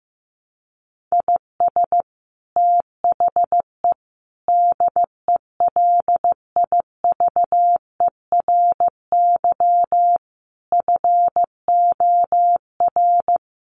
Morse code, heard as a keyed tone, about 15 wpm, 700 Hz. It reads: IS THE DELIVERY FOR
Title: Sentences from Top 1500 Words 15wpm